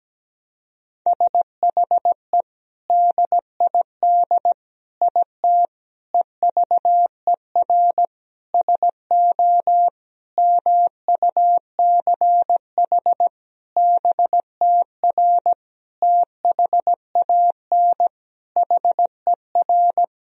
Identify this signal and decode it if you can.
Morse code, heard as a keyed tone, about 17 wpm, 700 Hz. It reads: SHE DID IT EVER SO MUCH BTR THAN HER
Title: Sentences from Top 400 Words and Abbreviations 17wpm